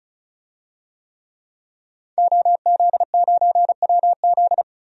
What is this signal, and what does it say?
Morse code, heard as a keyed tone, about 36 wpm, 700 Hz. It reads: OZ9WZ
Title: DXCC-100 Call Signs - 5 Characters - 35wpm